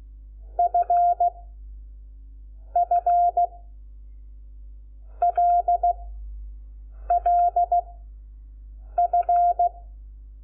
Morse code, heard as a keyed tone, about 16 wpm, 700 Hz. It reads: F F L L F
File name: Lecciones / CLASE Nº5